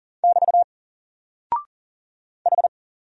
Morse code, heard as a keyed tone, about 40 wpm, 700 Hz. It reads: <BT> H
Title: Single Character (Letter:Number:Punctuation:Prosign) Alt 40wpm